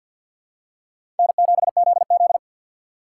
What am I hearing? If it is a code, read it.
Morse code, heard as a keyed tone, about 52 wpm, 700 Hz. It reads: N7ZZ
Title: USA Call Signs - 3x Character Spacing - 50wpm